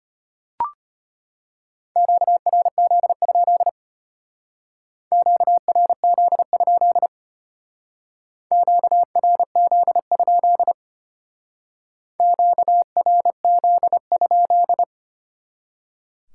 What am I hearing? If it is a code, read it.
Morse code, heard as a keyed tone, about 32 wpm, 700 Hz. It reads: QRZ? QRZ? QRZ? QRZ?